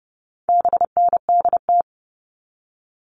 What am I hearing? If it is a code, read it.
Morse code, heard as a keyed tone, about 30 wpm, 700 Hz. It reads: BNDT
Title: All Cut Numbers - 4-Digit 30wpm